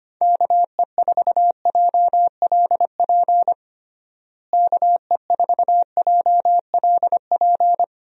Morse code, heard as a keyed tone, about 25 wpm, 700 Hz. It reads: KE4JLP KE4JLP
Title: USA Call Signs - 6 Characters - Repeated 3x - 25wpm